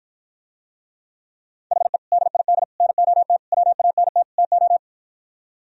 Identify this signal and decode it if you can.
Morse code, heard as a keyed tone, about 52 wpm, 700 Hz. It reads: HE DID NOT WANT TO